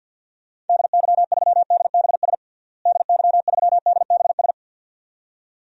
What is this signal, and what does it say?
Morse code, heard as a keyed tone, about 45 wpm, 700 Hz. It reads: DY3DBS DY3DBS
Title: DXCC-100 Call Signs - 6 Characters - Repeated 2x - 50wpm